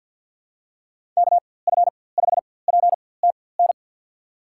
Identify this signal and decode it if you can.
Morse code, heard as a keyed tone, about 52 wpm, spaced as for 24 wpm, 700 Hz. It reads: KF5PTN
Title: USA Call Signs - 6 Characters - 4x Character Spacing - 50wpm